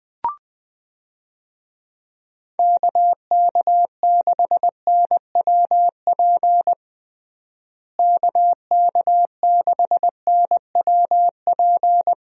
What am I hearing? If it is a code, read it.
Morse code, heard as a keyed tone, about 20 wpm, 700 Hz. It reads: KK6NWP KK6NWP
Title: USA Call Signs - 6 Characters - Repeated 2x - 20wpm